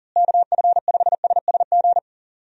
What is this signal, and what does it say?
Morse code, heard as a keyed tone, about 40 wpm, 700 Hz. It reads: KF5SSG
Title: USA Call Signs - 6 Characters - Repeated 3x - 40wpm